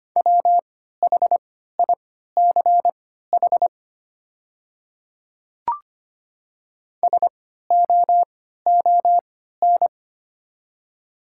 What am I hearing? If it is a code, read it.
Morse code, heard as a keyed tone, about 25 wpm, spaced as for 14 wpm, 690 Hz. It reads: WHICH SOON
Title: 3x Character Spacing - Top 500 Words - 25wpm